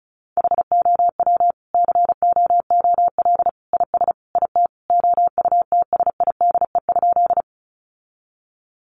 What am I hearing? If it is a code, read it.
Morse code, heard as a keyed tone, about 34 wpm, 700 Hz. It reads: HOW COOL IS IT OUTSIDE?
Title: Sentences from Top 1200 Words 35wpm